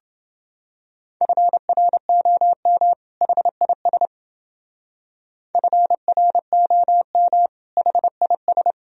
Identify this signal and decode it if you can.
Morse code, heard as a keyed tone, about 30 wpm, 700 Hz. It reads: FROM HIS FROM HIS